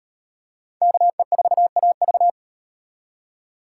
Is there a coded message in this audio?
Yes: Morse code from 0.8 to 2.3 s, heard as a keyed tone, about 34 wpm, 700 Hz. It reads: KE4AV